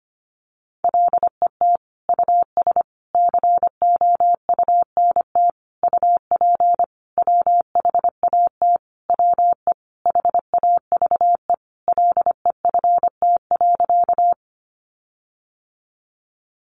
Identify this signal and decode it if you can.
Morse code, heard as a keyed tone, about 25 wpm, 700 Hz. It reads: LET US COUNT UP WHAT WE HAVE LEFT.
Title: Sentences from Top 1500 Words 25wpm